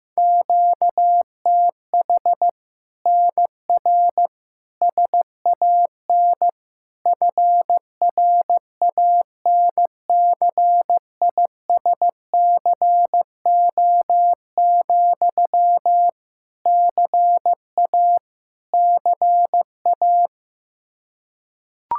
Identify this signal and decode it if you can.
Morse code, heard as a keyed tone, about 15 wpm, 700 Hz. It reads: QTH NR SAN FRANCISCO, CA CA
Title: Practice QSO - Exchange 1 - Beginning - 15wpm